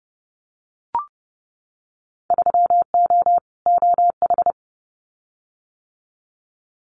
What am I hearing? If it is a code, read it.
Morse code, heard as a keyed tone, about 30 wpm, 700 Hz. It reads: 3O OH
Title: Winter Field Day Exchanges 30wpm